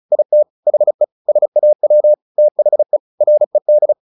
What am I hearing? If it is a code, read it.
Morse code, heard as a keyed tone, about 34 wpm, 600 Hz. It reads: IT HE SAW THE RED
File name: the old man and the sea / the old man and the sea 1 35wpm